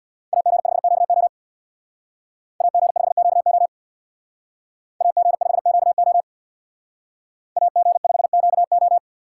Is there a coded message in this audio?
Yes: Morse code from 0.3 to 9.0 s, heard as a keyed tone, about 64 wpm, 700 Hz. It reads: AG5QO AG5QO AG5QO AG5QO